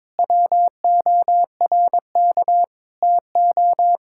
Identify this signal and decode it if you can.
Morse code, heard as a keyed tone, about 22 wpm, 700 Hz. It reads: WORK TO
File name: practice / 22wpm / Sentences with 8-Word Phrases - 22wpm